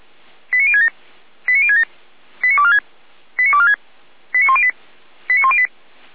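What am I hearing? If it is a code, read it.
Five-tone selective call (ZVEI), heard as quick runs of five tones, tones of about 70 ms. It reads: 89067 89067 89246 89246 89108 89108
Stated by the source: Five-tone ZVEI selective call received by SDR.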